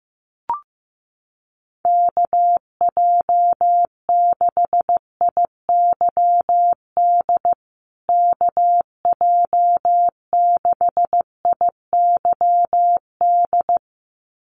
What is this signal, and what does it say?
Morse code, heard as a keyed tone, about 15 wpm, 700 Hz. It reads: KJ6IYD KJ6IYD
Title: USA Call Signs Repeated 2x 15wpm